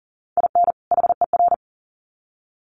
Morse code, heard as a keyed tone, about 40 wpm, 700 Hz. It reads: IN HER
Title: Sets of 2 Words 40wpm